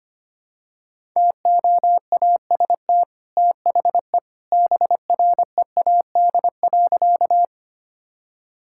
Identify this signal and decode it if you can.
Morse code, heard as a keyed tone, about 25 wpm, 700 Hz. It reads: TOAST THE BREAD.
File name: Easy Sentences with Hints - 25wpm